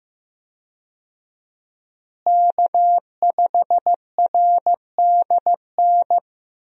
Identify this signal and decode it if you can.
Morse code, heard as a keyed tone, about 15 wpm, 700 Hz. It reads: K5RDN
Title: USA Call Signs - 5 Characters - Repeated 2x - 15wpm